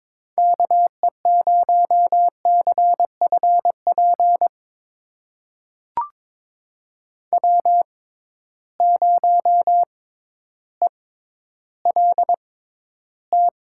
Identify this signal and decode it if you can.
Morse code, heard as a keyed tone, about 22 wpm, spaced as for 12 wpm, 700 Hz. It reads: KE0CFP W0ELT
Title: USA Call Signs - 6x Character Spacing - 22wpm